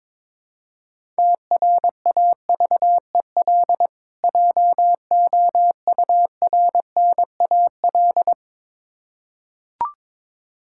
Morse code, heard as a keyed tone, about 22 wpm, 700 Hz. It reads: TRAVEL JOURNAL
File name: Difficult Words with Hints - 22wpm